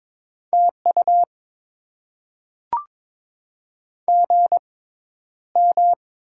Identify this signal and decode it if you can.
Morse code, heard as a keyed tone, about 22 wpm, spaced as for 11 wpm, 700 Hz. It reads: TU GM
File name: Top 20 QSO Elements - 6x Character Spacing - 22wpm